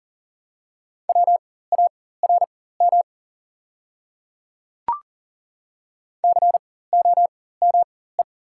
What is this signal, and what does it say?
Morse code, heard as a keyed tone, about 40 wpm, spaced as for 19 wpm, 690 Hz. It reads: WARM COME
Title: 4x Character Spacing - Top 500 Words - 40wpm